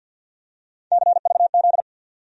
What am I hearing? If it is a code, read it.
Morse code, heard as a keyed tone, about 52 wpm, 700 Hz. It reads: CVZ
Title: USA Call Sign Postfixes - 3 Letters 50wpm